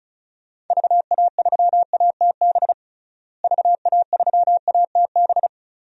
Morse code, heard as a keyed tone, about 36 wpm, 700 Hz. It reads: VA3ATB VA3ATB
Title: DXCC-100 Call Signs - 6 Characters - Repeated 3x - 35wpm